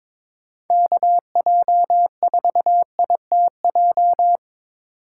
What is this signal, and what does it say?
Morse code, heard as a keyed tone, about 25 wpm, 700 Hz. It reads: KJ4ITJ
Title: Speed Racing - USA 2x3 Call Signs 22wpm